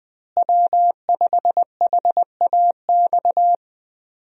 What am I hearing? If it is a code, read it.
Morse code, heard as a keyed tone, about 20 wpm, 700 Hz. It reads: W5HAX
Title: USA Call Signs - 5 Characters - Repeated 3x - 20wpm